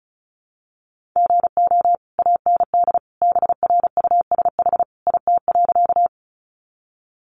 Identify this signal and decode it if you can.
Morse code, heard as a keyed tone, about 34 wpm, 700 Hz. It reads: GO AND BRUSH IT.
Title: Sentences from Top 1100 Words 35wpm